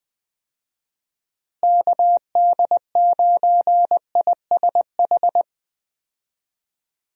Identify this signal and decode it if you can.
Morse code, heard as a keyed tone, about 20 wpm, 700 Hz. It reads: KD9ISH
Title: USA Call Signs - 6 Characters - Repeated 3x - 20wpm